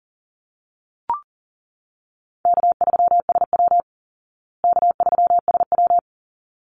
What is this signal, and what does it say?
Morse code, heard as a keyed tone, about 40 wpm, 700 Hz. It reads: K3SW K3SW
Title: Call Signs Repeated 3x - 40wpm